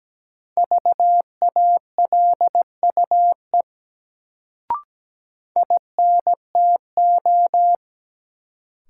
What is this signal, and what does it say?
Morse code, heard as a keyed tone, about 17 wpm, 700 Hz. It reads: VALUE INTO
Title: Sentences from Top 600 Words - New Word Drill - 17wpm